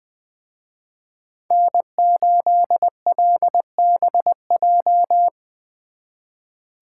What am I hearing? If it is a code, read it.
Morse code, heard as a keyed tone, about 24 wpm, 700 Hz. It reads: N8LBJ
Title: Speed Racing - USA 1x3 Call Signs 20wpm